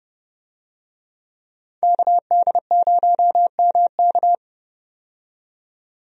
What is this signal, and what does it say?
Morse code, heard as a keyed tone, about 29 wpm, 700 Hz. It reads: KD0MK
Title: USA Call Signs - 5 Characters - 30wpm